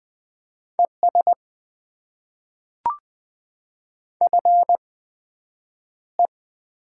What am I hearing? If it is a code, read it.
Morse code, heard as a keyed tone, about 20 wpm, spaced as for 9 wpm, 700 Hz. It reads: ES FE
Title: Top 20 QSO Elements - 8x Character Spacing - 20wpm